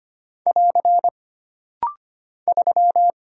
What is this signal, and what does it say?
Morse code, heard as a keyed tone, about 25 wpm, 700 Hz. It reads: <AR> 3
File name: Single Character (Letter:Number:Punctuation:Prosign) - Rapid-Fire - 25wpm